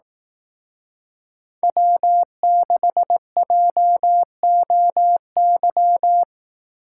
Morse code, heard as a keyed tone, about 18 wpm, 700 Hz. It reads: W6JOY
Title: Speed Racing - USA 1x3 Call Signs 15wpm